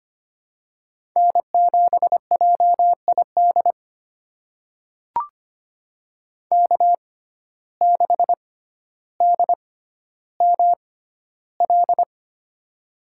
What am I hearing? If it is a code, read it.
Morse code, heard as a keyed tone, about 25 wpm, spaced as for 13 wpm, 700 Hz. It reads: N7JID K6DML
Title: USA Call Signs - 5 Characters - 6x Character Spacing - 25wpm